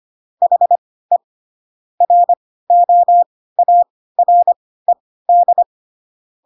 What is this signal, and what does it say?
Morse code, heard as a keyed tone, about 25 wpm, spaced as for 16 wpm, 700 Hz. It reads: HE ROARED
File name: Pinocchio / Pinocchio ch3 10 25 ik1hge